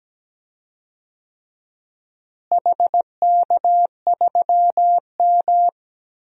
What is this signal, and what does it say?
Morse code, heard as a keyed tone, about 17 wpm, 700 Hz. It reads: HK3M
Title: DXCC-100 Call Signs - 4 Characters - Repeated 3x - 17wpm